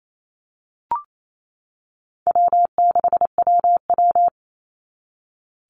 Morse code, heard as a keyed tone, about 28 wpm, 700 Hz. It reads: W6WW
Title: NA SOTA Chaser Callsigns 28wpm